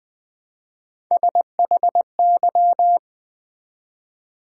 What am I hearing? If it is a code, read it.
Morse code, heard as a keyed tone, about 20 wpm, 690 Hz. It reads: SHY
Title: Build A Word - 1 Letter At A Time - 3-Letter Words - 20wpm